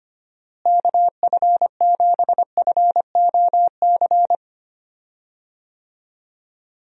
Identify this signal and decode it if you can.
Morse code, heard as a keyed tone, about 25 wpm, 700 Hz. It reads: KF7FOC
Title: USA Call Signs - 6 Characters - Repeated 2x - 25wpm